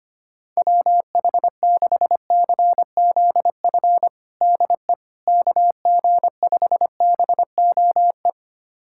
Morse code, heard as a keyed tone, about 25 wpm, 700 Hz. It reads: WH6CZF DE KG5BOE
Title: USA Callsign DE Callsign - 6 Characters - 25wpm